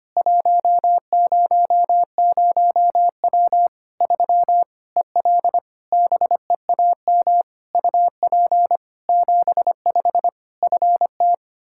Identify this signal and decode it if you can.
Morse code, heard as a keyed tone, about 25 wpm, 700 Hz. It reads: 100W 3 EL BEAM UP 75 FT
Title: K1USN SST QSO Practice 25wpm